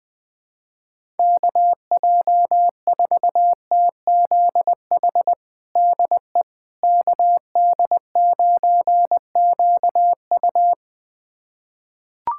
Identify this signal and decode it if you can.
Morse code, heard as a keyed tone, about 20 wpm, 700 Hz. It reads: KJ4TZH DE KD9QU
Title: USA Callsign DE Callsign 20wpm